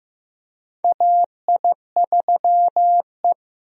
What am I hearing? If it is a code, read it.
Morse code, heard as a keyed tone, about 16 wpm, 700 Hz. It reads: AI3E
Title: USA Call Signs - 4 Characters - Repeated 3x - 15wpm